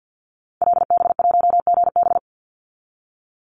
Morse code, heard as a keyed tone, about 48 wpm, 700 Hz. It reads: LB1GB
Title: Call Signs Repeated 3x - 50wpm